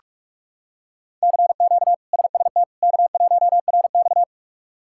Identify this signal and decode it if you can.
Morse code, heard as a keyed tone, about 41 wpm, 700 Hz. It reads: CQ SST K1RX
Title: K1USN SST QSO Practice - 45wpm